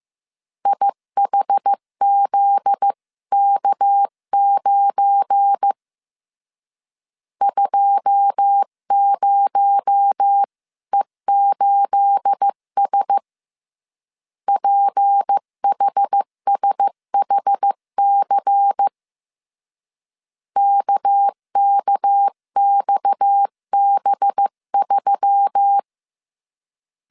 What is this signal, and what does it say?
Morse code, heard as a keyed tone, about 15 wpm, 800 Hz. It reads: IHZK9 20E8S PHSHC KKXB3
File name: CW / DET / 1 9 1500C S B 14wpm 3302